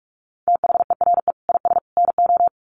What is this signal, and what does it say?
Morse code, heard as a keyed tone, about 44 wpm, 700 Hz. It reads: THERE IS NO
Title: Sets of 3 Words 45wpm